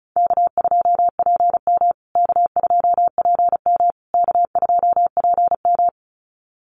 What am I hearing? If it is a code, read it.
Morse code, heard as a keyed tone, about 34 wpm, 700 Hz. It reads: K2PM K2PM K2PM
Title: USA Call Signs Repeated 3x 35wpm